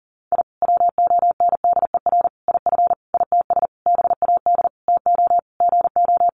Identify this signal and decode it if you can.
Morse code, heard as a keyed tone, about 40 wpm, 700 Hz. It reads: I WONDER IF ITS BAD TO GO
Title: Sentences from Top 700 Words 40wpm